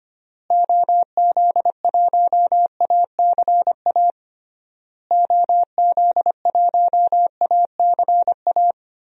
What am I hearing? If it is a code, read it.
Morse code, heard as a keyed tone, about 25 wpm, 700 Hz. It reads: OZ1ACA OZ1ACA
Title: DXCC-100 Call Signs - 6 Characters - Repeated 3x - 25wpm